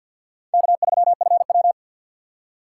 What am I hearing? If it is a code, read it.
Morse code, heard as a keyed tone, about 48 wpm, 700 Hz. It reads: K3FW
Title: USA Call Signs - 4 Characters - 3x Character Spacing - 50wpm